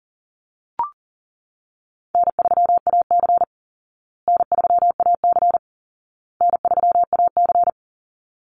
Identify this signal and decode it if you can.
Morse code, heard as a keyed tone, about 40 wpm, 700 Hz. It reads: N3AC N3AC N3AC
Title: Easy Call Signs Repeated 3x - 40wpm